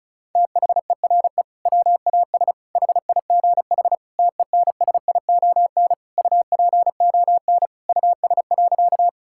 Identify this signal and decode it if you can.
Morse code, heard as a keyed tone, about 34 wpm, 700 Hz. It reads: THERE WAS HIGH TENSION UPON US.
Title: Sentences from Top 1600 Words 35wpm